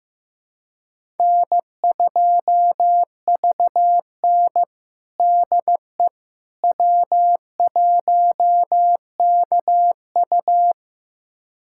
Morse code, heard as a keyed tone, about 15 wpm, 700 Hz. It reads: N2VN DE W1KU
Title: USA Callsign DE Callsign 15wpm